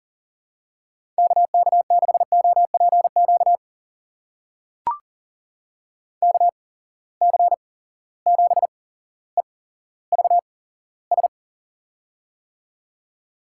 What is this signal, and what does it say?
Morse code, heard as a keyed tone, about 40 wpm, spaced as for 17 wpm, 700 Hz. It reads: KK6OPQ KC7EVS
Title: USA Call Signs - 6 Characters - 8x Character Spacing - 40wpm